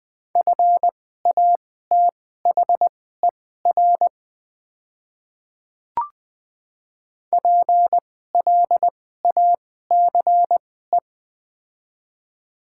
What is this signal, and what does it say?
Morse code, heard as a keyed tone, about 20 wpm, spaced as for 14 wpm, 700 Hz. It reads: FATHER PLACE
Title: 2x Character Spacing - Top 200 Words - 20wpm